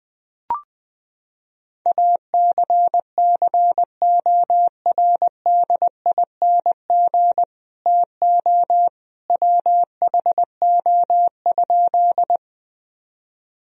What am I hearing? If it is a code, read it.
Morse code, heard as a keyed tone, about 21 wpm, 700 Hz. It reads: ACCORDING TO WHO?
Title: Sentences from Top 1000 Words - Review Part 4 - 20wpm